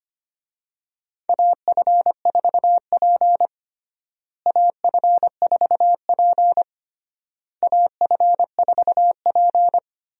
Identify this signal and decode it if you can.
Morse code, heard as a keyed tone, about 25 wpm, 700 Hz. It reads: AF4P AF4P AF4P
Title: USA Call Signs - 4 Characters - Repeated 3x - 25wpm